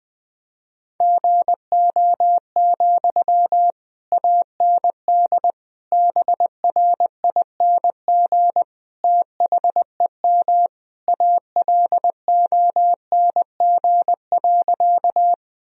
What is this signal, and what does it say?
Morse code, heard as a keyed tone, about 20 wpm, 700 Hz. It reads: GO, AND BRING THEM ALONG.
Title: Sentences from Top 1000 Words - Review Part 3 - 20wpm